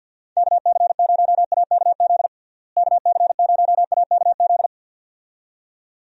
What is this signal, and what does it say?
Morse code, heard as a keyed tone, about 52 wpm, 700 Hz. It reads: KC0AKZ KC0AKZ
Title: USA Call Signs - 6 Characters - Repeated 2x - 50wpm